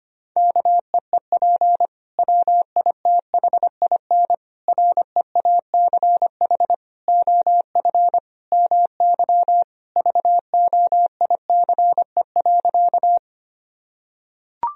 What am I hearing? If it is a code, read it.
Morse code, heard as a keyed tone, about 25 wpm, 700 Hz. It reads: KEEP WITHIN REACH OF MY VOICE.
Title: Sentences from Top 1000 Words - Review Part 4 - 25wpm